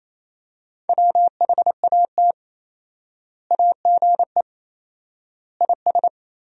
Morse code, heard as a keyed tone, about 28 wpm, 700 Hz. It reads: WHAT AGE IS
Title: 4x Word-Spacing - Sentences from Top 500 Words - 28wpm